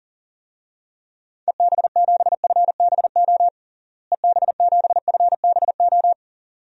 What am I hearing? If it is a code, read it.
Morse code, heard as a keyed tone, about 37 wpm, 700 Hz. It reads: EB7FBO EB7FBO
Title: DXCC-100 Call Signs - 6 Characters - Repeated 2x - 40wpm